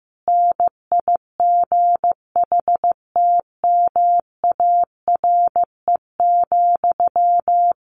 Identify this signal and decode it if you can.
Morse code, heard as a keyed tone, about 15 wpm, 700 Hz. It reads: NIGHTMARE,
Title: A Sound Of Thunder 15wpm